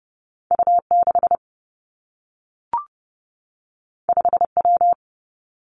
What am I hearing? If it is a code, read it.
Morse code, heard as a keyed tone, about 30 wpm, 700 Hz. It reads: U 6 5 W
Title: Two Challenging Letter-Number Combinations 30wpm